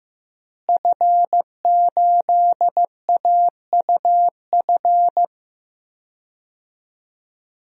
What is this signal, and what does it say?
Morse code, heard as a keyed tone, about 15 wpm, 700 Hz. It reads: F8AUF
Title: DXCC-100 Call Signs - 5 Characters - Repeated 2x - 15wpm